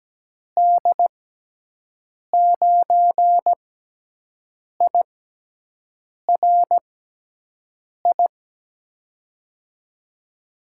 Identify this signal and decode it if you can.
Morse code, heard as a keyed tone, about 17 wpm, spaced as for 6 wpm, 700 Hz. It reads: D9IRI
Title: USA Call Signs - 6 Characters - 6x Character Spacing - 17wpm